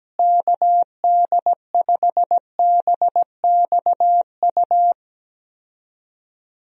Morse code, heard as a keyed tone, about 17 wpm, 700 Hz. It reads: KD5BXU
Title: USA Call Signs - 6 Characters - Repeated 3x - 17wpm